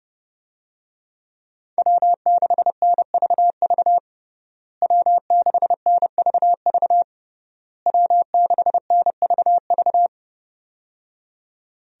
Morse code, heard as a keyed tone, about 30 wpm, 700 Hz. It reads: W6NVV W6NVV W6NVV
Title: USA Call Signs - 5 Characters - Repeated 3x - 30wpm